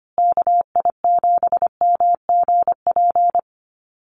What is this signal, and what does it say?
Morse code, heard as a keyed tone, about 25 wpm, 700 Hz. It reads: KI7MGP
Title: USA Call Signs Repeated 2x 25wpm